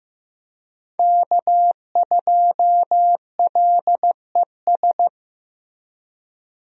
Morse code, heard as a keyed tone, about 15 wpm, 700 Hz. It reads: K2LES
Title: USA Call Signs - 5 Characters - Repeated 2x - 15wpm